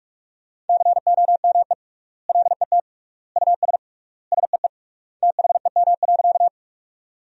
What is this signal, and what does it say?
Morse code, heard as a keyed tone, about 43 wpm, 700 Hz. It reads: COME LET US SEE THEM.
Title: 3x Word-Spacing - Sentences from Top 300 Words - 45wpm